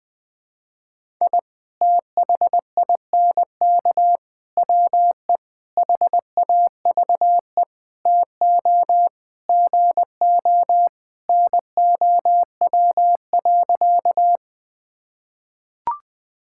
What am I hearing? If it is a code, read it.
Morse code, heard as a keyed tone, about 20 wpm, 700 Hz. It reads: I THINK WE HAVE TO GO NOW.
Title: Sentences with 5-Word Phrases - 20wpm